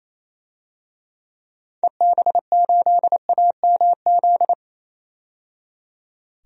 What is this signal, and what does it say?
Morse code, heard as a keyed tone, about 28 wpm, 700 Hz. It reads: EB8AMZ
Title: DXCC-100 Call Signs - 6 Characters - 28wpm